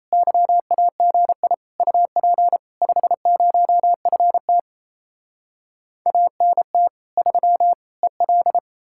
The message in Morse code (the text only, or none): YAGI UP 50FT ANT 3 EL